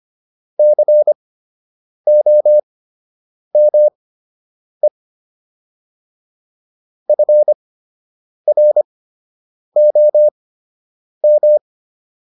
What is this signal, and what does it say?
Morse code, heard as a keyed tone, about 25 wpm, spaced as for 8 wpm, 600 Hz. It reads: COME FROM